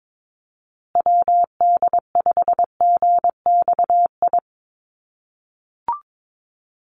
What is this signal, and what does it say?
Morse code, heard as a keyed tone, about 22 wpm, 700 Hz. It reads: WD5GXI
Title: USA Call Signs Repeated 2x 22wpm